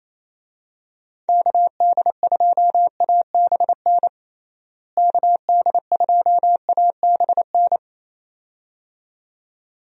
Morse code, heard as a keyed tone, about 28 wpm, 700 Hz. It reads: KD2ABN KD2ABN
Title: USA Call Signs - 6 Characters - Repeated 2x - 28wpm